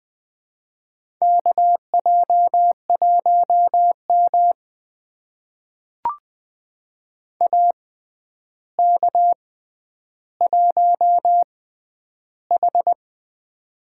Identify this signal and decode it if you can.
Morse code, heard as a keyed tone, about 20 wpm, spaced as for 10 wpm, 700 Hz. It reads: KJ1M AK1H
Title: USA Call Signs - 4 Characters - 6x Character Spacing - 20wpm